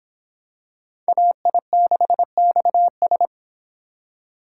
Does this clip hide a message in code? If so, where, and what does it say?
Yes: Morse code from 1.1 to 3.3 s, heard as a keyed tone, about 26 wpm, 700 Hz. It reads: AI6XS